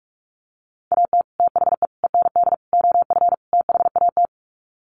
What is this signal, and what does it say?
Morse code, heard as a keyed tone, about 44 wpm, 700 Hz. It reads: AT THE END OF THAT
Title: Sets of 5 Words Encore2 45wpm